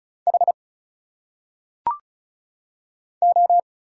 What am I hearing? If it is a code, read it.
Morse code, heard as a keyed tone, about 35 wpm, 700 Hz. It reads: H O
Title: Single Letters - Phonetic Pronunciation 35wpm